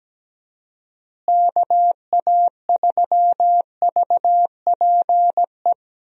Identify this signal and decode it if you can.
Morse code, heard as a keyed tone, about 17 wpm, 700 Hz. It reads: KA3VPE
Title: USA Call Signs - 6 Characters - Repeated 3x - 17wpm